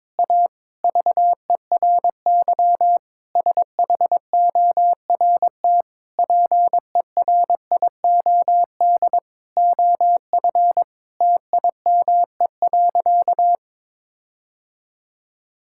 Morse code, heard as a keyed tone, about 22 wpm, 700 Hz. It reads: A VERY SHORT PERIOD OF TIME.
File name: Sentences with 7-Word Phrases - 22wpm